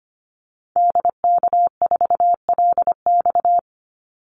Morse code, heard as a keyed tone, about 25 wpm, 700 Hz. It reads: DK4LX
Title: Call Signs - 25wpm